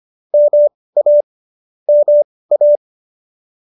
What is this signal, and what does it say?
Morse code, heard as a keyed tone, about 25 wpm, spaced as for 18 wpm, 590 Hz. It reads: MA MA
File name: qso201 18